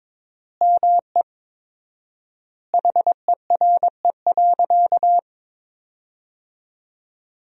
Morse code, heard as a keyed tone, about 21 wpm, 700 Hz. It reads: ME HERE.
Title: 4x Word-Spacing - Sentences from Top 200 Words - 22wpm